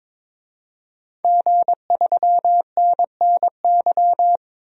Morse code, heard as a keyed tone, about 22 wpm, 700 Hz. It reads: G3NNY
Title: DXCC-100 Call Signs - 5 Characters - 22wpm